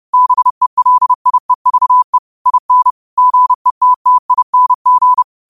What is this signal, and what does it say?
Morse code, heard as a keyed tone, about 30 wpm, 1000 Hz. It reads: BELIEVE IN GETTING
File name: 30wpm